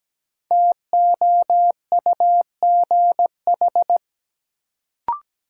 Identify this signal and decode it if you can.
Morse code, heard as a keyed tone, about 17 wpm, 700 Hz. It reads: TOUGH
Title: Sentences from Top 1200 Words - New Word Drill Alt - 17wpm